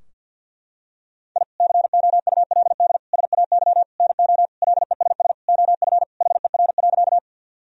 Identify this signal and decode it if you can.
Morse code, heard as a keyed tone, about 50 wpm, 700 Hz. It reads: I COULD SAY NO LESS OF HER.
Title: Sentences from Top 1000 Words - Review Part 2 - 50wpm